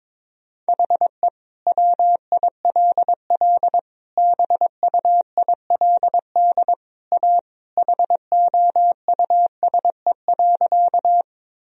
Morse code, heard as a keyed tone, about 22 wpm, 700 Hz. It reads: HE WILL BUILD A HOUSE.
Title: Sentences from Top 1000 Words - Review Part 1 - 22wpm